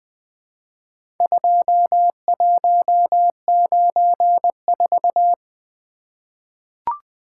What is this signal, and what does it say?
Morse code, heard as a keyed tone, about 20 wpm, 700 Hz. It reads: 2194
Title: Four Letter-Number Combinations - 20wpm